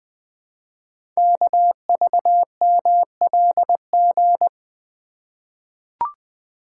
Morse code, heard as a keyed tone, about 20 wpm, 700 Hz. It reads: KVMLG
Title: Code Groups - 5 Letters - 20wpm